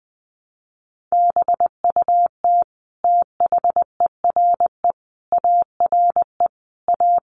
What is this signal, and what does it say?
Morse code, heard as a keyed tone, about 20 wpm, 700 Hz. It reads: BUT THERE ARE A
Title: Sets of 5 Words Encore2 20wpm